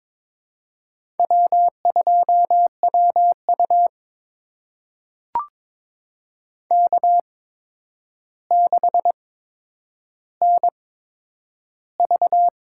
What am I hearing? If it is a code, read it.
Morse code, heard as a keyed tone, about 22 wpm, spaced as for 9 wpm, 700 Hz. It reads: W2WU K6NV
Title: USA Call Signs - 4 Characters - 8x Character Spacing - 22wpm